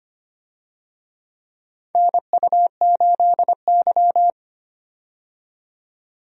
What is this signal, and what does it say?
Morse code, heard as a keyed tone, about 25 wpm, 700 Hz. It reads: NU8Y
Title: USA Call Signs - 4 Characters - 25wpm